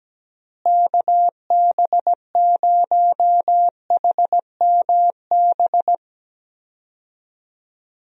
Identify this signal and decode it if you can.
Morse code, heard as a keyed tone, about 17 wpm, 700 Hz. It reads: KB0HMB
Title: USA Call Signs - 6 Characters - Repeated 2x - 17wpm